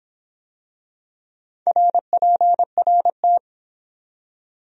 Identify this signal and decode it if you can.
Morse code, heard as a keyed tone, about 22 wpm, 700 Hz. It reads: RPRT